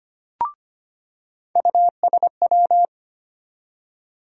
Morse code, heard as a keyed tone, about 25 wpm, 700 Hz. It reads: USW
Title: USA Call Sign Postfixes - 3 Letters 25wpm